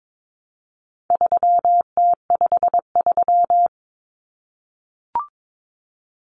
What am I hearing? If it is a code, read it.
Morse code, heard as a keyed tone, about 22 wpm, 700 Hz. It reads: 3T53
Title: Common Cut Numbers - 4-Digit 22wpm